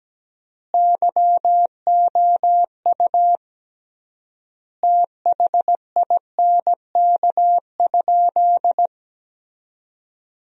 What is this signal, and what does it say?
Morse code, heard as a keyed tone, about 17 wpm, 700 Hz. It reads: YOU THINK?
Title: 3x Word-Spacing - Sentences from Top 200 Words - 17wpm